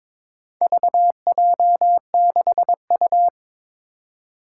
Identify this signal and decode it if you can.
Morse code, heard as a keyed tone, about 22 wpm, 700 Hz. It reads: VJ6U
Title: Four Challenging Letter-Number Combinations 22wpm